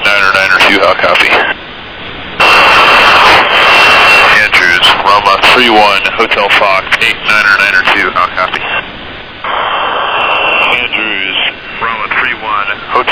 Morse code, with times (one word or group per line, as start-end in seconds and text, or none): none